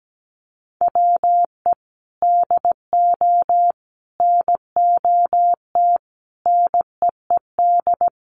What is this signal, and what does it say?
Morse code, heard as a keyed tone, about 17 wpm, 700 Hz. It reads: WE DO NOT NEED
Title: Sets of 5 Words Encore 17wpm